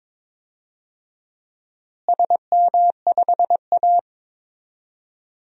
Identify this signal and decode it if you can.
Morse code, heard as a keyed tone, about 22 wpm, 700 Hz. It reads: SM5A
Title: DXCC-100 Call Signs - 4 Characters - Repeated 3x - 22wpm